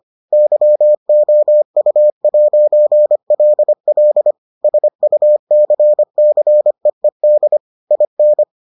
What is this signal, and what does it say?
Morse code, heard as a keyed tone, about 25 wpm, 600 Hz. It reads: YOU'LL SUCCEED IN